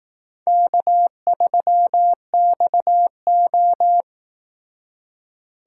Morse code, heard as a keyed tone, about 18 wpm, 700 Hz. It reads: K3XO
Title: Speed Racing - USA 1x2 Call Signs 15wpm